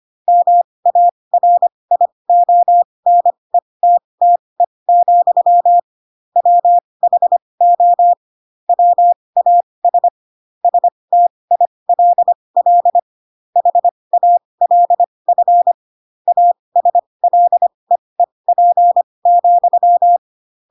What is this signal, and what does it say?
Morse code, heard as a keyed tone, about 25 wpm, spaced as for 20 wpm, 700 Hz. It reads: MARIONETTE, WHO WAS STILL HALF ASLEEP,
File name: Pinocchio / Pinocchio ch7 15 25 ik1hge